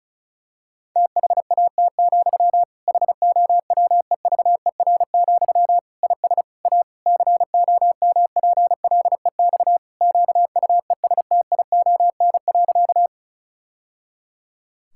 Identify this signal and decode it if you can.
Morse code, heard as a keyed tone, about 36 wpm, 700 Hz. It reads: THAT, HOWEVER, IS A COMPLEX QUESTION.
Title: Sentences from Top 1000 Words - Review Part 6 - 35wpm